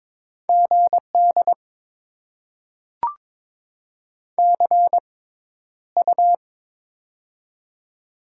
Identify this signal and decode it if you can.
Morse code, heard as a keyed tone, about 22 wpm, spaced as for 11 wpm, 700 Hz. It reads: GD CU
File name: Top 90 QSO Elements - 6x Character Spacing - 22wpm